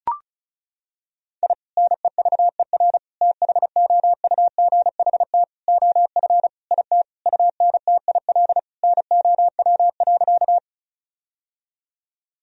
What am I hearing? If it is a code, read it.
Morse code, heard as a keyed tone, about 34 wpm, 700 Hz. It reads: I NEVER THOUGHT OF IT UNTIL NOW.
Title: Sentences from Top 1000 Words - Review Part 1 - 35wpm